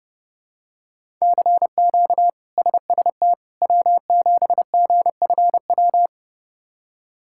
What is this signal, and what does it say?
Morse code, heard as a keyed tone, about 32 wpm, 700 Hz. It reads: CQ SST W7GFW
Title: K1USN SST QSO Practice - 30wpm